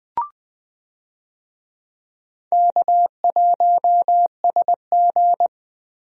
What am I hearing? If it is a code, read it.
Morse code, heard as a keyed tone, about 20 wpm, 700 Hz. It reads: K1SG
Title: USA Call Signs - 4 Characters - Repeated 2x - 20wpm